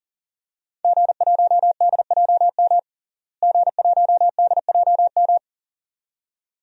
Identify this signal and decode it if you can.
Morse code, heard as a keyed tone, about 40 wpm, 700 Hz. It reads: G1DJM G1DJM
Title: DXCC-100 Call Signs - 5 Characters - Repeated 2x - 40wpm